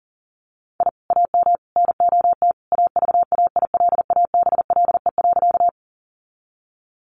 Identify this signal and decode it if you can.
Morse code, heard as a keyed tone, about 40 wpm, 700 Hz. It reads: I AM NOT AVAILABLE.
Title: Sentences from Top 600 Words 40wpm